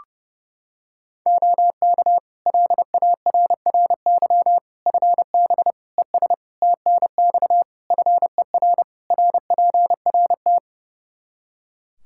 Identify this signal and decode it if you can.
Morse code, heard as a keyed tone, about 30 wpm, 700 Hz. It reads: OK LARRY FB ES TNX FER RPRT
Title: Practice QSO - Exchange 2 - Info - 30wpm